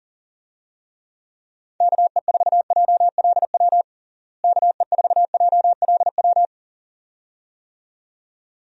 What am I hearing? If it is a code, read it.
Morse code, heard as a keyed tone, about 40 wpm, 700 Hz. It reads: KE4JLW KE4JLW
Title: USA Call Signs - 6 Characters - Repeated 2x - 40wpm